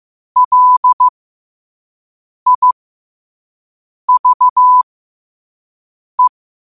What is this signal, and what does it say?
Morse code, heard as a keyed tone, about 15 wpm, spaced as for 5 wpm, 1000 Hz. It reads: LIVE
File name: qotdfiles / 5wpm